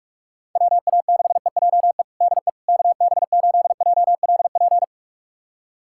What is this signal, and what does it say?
Morse code, heard as a keyed tone, about 46 wpm, 700 Hz. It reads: WA6EJE DE KB8JLP
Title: USA Callsign DE Callsign - 6 Characters - 45wpm